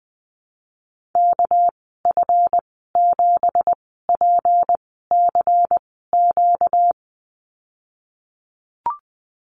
Morse code, heard as a keyed tone, about 20 wpm, spaced as for 15 wpm, 700 Hz. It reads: KF7PCQ
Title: 238 Lesson 38g - Letter q - Call Signs 20wpm